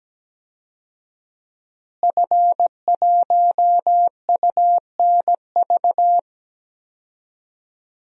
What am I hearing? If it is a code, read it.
Morse code, heard as a keyed tone, about 17 wpm, 700 Hz. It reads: F1UNV
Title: DXCC-100 Call Signs - 5 Characters - Repeated 3x - 17wpm